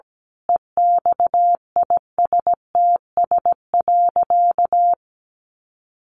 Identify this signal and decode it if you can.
Morse code, heard as a keyed tone, about 17 wpm, 700 Hz. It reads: EXISTS.
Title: A Sound Of Thunder 17wpm